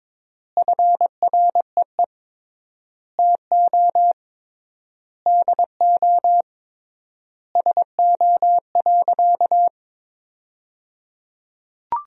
Morse code, heard as a keyed tone, about 22 wpm, 700 Hz. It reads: FREE TO DO SO.
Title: 3x Word-Spacing - Sentences from Top 500 Words - 22wpm